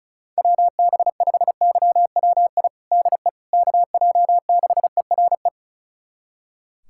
Morse code, heard as a keyed tone, about 36 wpm, 700 Hz. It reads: WB5YWI DE KJ6ERE
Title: USA Callsign DE Callsign - 6 Characters - 35wpm